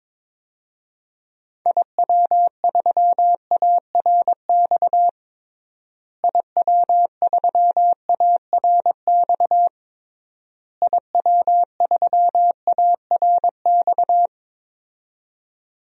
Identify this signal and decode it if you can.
Morse code, heard as a keyed tone, about 23 wpm, 700 Hz. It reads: IW3ARX IW3ARX IW3ARX
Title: DXCC-100 Call Signs - 6 Characters - Repeated 3x - 22wpm